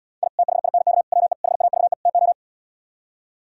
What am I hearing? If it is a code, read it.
Morse code, heard as a keyed tone, about 77 wpm, 690 Hz. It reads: I THINK WE HAVE TO